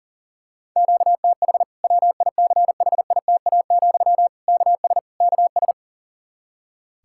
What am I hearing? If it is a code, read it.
Morse code, heard as a keyed tone, about 37 wpm, 700 Hz. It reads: QTH WICHITA, KS KS
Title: Practice QSO - Exchange 1 - Beginning - 40wpm